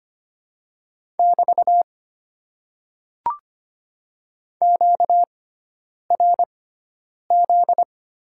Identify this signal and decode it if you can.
Morse code, heard as a keyed tone, about 25 wpm, spaced as for 9 wpm, 700 Hz. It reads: <BT> QRZ
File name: Top 90 QSO Elements - 6x Character Spacing - 25wpm